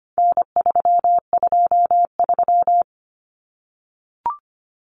Morse code, heard as a keyed tone, about 25 wpm, 700 Hz. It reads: N323
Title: Common Cut Numbers - 4-Digit 25wpm